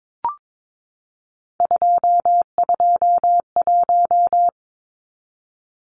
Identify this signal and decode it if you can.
Morse code, heard as a keyed tone, about 22 wpm, 700 Hz. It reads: 221
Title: Numbers - 3-Digit 22wpm